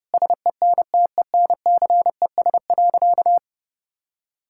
Morse code, heard as a keyed tone, about 30 wpm, 700 Hz. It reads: SENTENCES.
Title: Pangram Challenge v2 - 30wpm